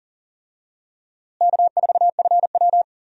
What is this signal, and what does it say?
Morse code, heard as a keyed tone, about 37 wpm, 700 Hz. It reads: K4FW
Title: USA Call Signs - 4 Characters - Repeated 2x - 40wpm